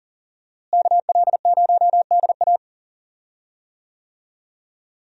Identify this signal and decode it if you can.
Morse code, heard as a keyed tone, about 40 wpm, 700 Hz. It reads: KL0DA
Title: USA Call Signs - 5 Characters - Repeated 2x - 40wpm